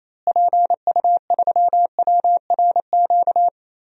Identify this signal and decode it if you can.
Morse code, heard as a keyed tone, about 28 wpm, 700 Hz. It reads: PU3WRQ
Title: DXCC-100 Call Signs - 6 Characters - Repeated 2x - 28wpm